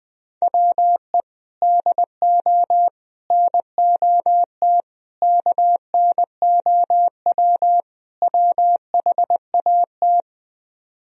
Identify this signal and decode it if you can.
Morse code, heard as a keyed tone, about 20 wpm, 700 Hz. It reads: WE DO NOT KNOW WHAT
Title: Speed-Racing - Sets of 5 Words 20wpm